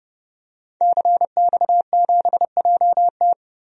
Morse code, heard as a keyed tone, about 30 wpm, 700 Hz. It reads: CX7JT
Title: DXCC-100 Call Signs - 5 Characters - 30wpm